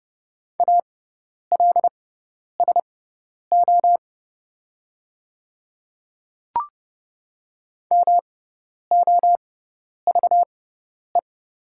Morse code, heard as a keyed tone, about 30 wpm, spaced as for 10 wpm, 690 Hz. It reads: ALSO MOVE
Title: 6x Character Spacing - Top 200 Words - 30wpm